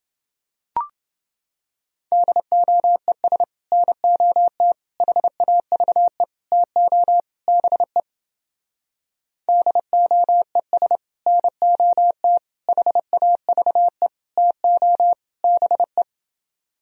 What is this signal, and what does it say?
Morse code, heard as a keyed tone, about 30 wpm, 700 Hz. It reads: DOES NOT HAVE TO BE DOES NOT HAVE TO BE